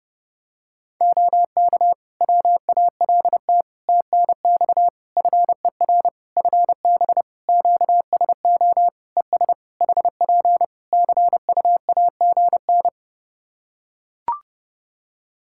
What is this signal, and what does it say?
Morse code, heard as a keyed tone, about 30 wpm, 700 Hz. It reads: OK WALT TNX FER FB QSO ES HP CUAGN
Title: Practice QSO - Exchange 4 - Ending - 30wpm